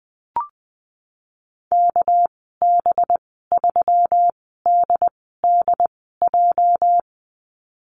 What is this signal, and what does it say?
Morse code, heard as a keyed tone, about 19 wpm, spaced as for 14 wpm, 700 Hz. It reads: KB3DDJ
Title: 231 Lesson 37g - Letter x - Call Signs 20wpm